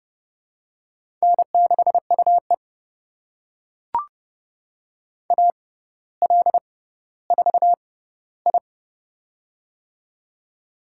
Morse code, heard as a keyed tone, about 30 wpm, spaced as for 15 wpm, 700 Hz. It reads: N6UE AL4I
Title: USA Call Signs - 4 Characters - 6x Character Spacing - 30wpm